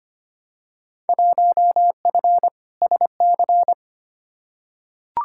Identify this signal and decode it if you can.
Morse code, heard as a keyed tone, about 25 wpm, 700 Hz. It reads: 1F SC
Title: Field Day Exchanges Encore - 25wpm